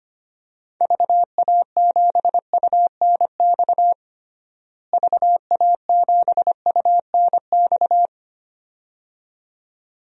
Morse code, heard as a keyed tone, about 25 wpm, 700 Hz. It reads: VA7UNX VA7UNX
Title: DXCC-100 Call Signs - 6 Characters - Repeated 2x - 25wpm